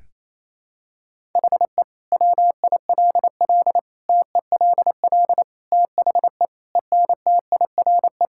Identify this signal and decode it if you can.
Morse code, heard as a keyed tone, about 27 wpm, 700 Hz. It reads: HE WILL TELL THE ENTIRE
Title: Sentences from Top 1000 Words - Review Part 5 - 28wpm